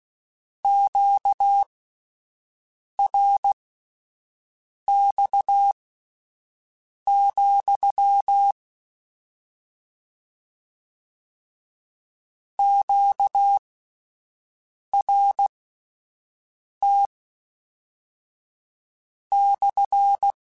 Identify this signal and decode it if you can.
Morse code, heard as a keyed tone, about 16 wpm, spaced as for 5 wpm, 800 Hz. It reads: QRX, QRT /
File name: Q Calls / Q Calls-mp3s / Q CALLS SPSL @10